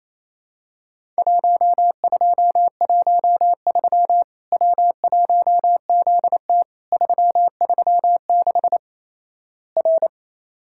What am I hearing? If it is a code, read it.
Morse code, heard as a keyed tone, about 28 wpm, 700 Hz. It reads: 1213 W1ZT 336 R
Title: Worked All Europe DX Contest - QTC Exchange - 28wpm